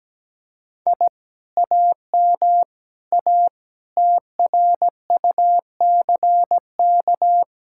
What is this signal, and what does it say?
Morse code, heard as a keyed tone, about 17 wpm, 700 Hz. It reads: I AM A TRUCK
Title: Speed-Racing Common QSO Phrases - 17wpm